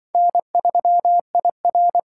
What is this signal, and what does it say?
Morse code, heard as a keyed tone, about 24 wpm, 700 Hz. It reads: N3IR
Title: Speed Racing - USA 1x2 Call Signs 20wpm